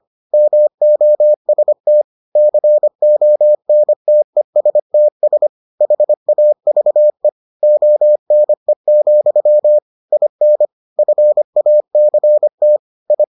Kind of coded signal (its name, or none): Morse code